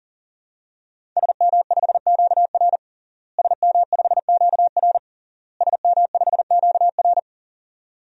Morse code, heard as a keyed tone, about 40 wpm, 700 Hz. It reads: SM5QR SM5QR SM5QR
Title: DXCC-100 Call Signs - 5 Characters - Repeated 3x - 40wpm